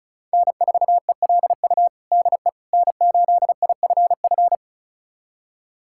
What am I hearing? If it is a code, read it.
Morse code, heard as a keyed tone, about 36 wpm, 700 Hz. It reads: N4ELU DE N8IFF
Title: USA Callsign DE Callsign - 5 Characters - 35wpm